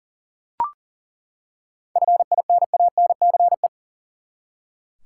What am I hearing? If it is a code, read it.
Morse code, heard as a keyed tone, about 40 wpm, 700 Hz. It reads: FINANCE
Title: Seven Letter Words - 40wpm